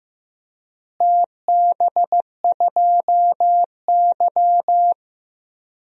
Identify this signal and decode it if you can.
Morse code, heard as a keyed tone, about 15 wpm, 700 Hz. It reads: TB2Y
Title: DXCC-100 Call Signs - 4 Characters - Repeated 2x - 15wpm